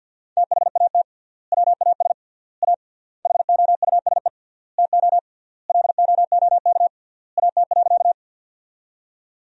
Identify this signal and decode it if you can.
Morse code, heard as a keyed tone, about 52 wpm, 700 Hz. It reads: THAT WAS A HOUSE TO LOOK AT.
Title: 3x Word-Spacing - Sentences from Top 200 Words - 50wpm